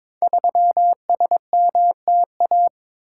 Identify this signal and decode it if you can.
Morse code, heard as a keyed tone, about 22 wpm, 700 Hz. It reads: 3SMTA
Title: Code Groups - 5 Letter:Numbers - 22wpm